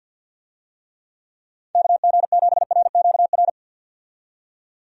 Morse code, heard as a keyed tone, about 49 wpm, 700 Hz. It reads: KG7RQR
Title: USA Call Signs - 6 Characters - 50wpm